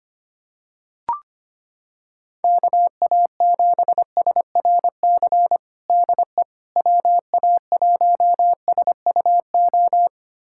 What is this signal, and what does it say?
Morse code, heard as a keyed tone, about 25 wpm, 700 Hz. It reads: KA7SRC DE WA1SUO
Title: USA Callsign DE Callsign - 6 Characters - 25wpm